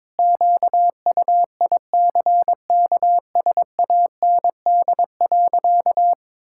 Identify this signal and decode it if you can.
Morse code, heard as a keyed tone, about 22 wpm, 700 Hz. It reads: QUICKSAND.
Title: Pangram Challenge v2 - 22wpm